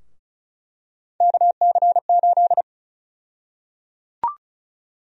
Morse code, heard as a keyed tone, about 34 wpm, 700 Hz. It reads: KC8
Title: USA Call Sign Prefixes - Repeated 2x - 35wpm